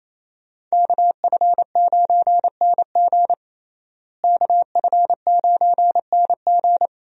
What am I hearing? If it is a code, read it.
Morse code, heard as a keyed tone, about 28 wpm, 700 Hz. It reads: KF9NG KF9NG
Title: USA Call Signs - 5 Characters - Repeated 3x - 28wpm